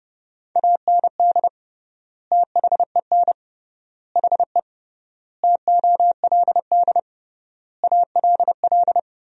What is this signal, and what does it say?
Morse code, heard as a keyed tone, about 32 wpm, 700 Hz. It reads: AND THEN HE TOLD ALL
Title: 3x Word-Spacing - Sentences from Top 400 Words - 30wpm